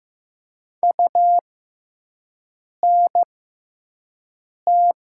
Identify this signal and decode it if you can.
Morse code, heard as a keyed tone, about 15 wpm, spaced as for 5 wpm, 700 Hz. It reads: UNT
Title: 6x Character Spacing - Top 300 Words - 15wpm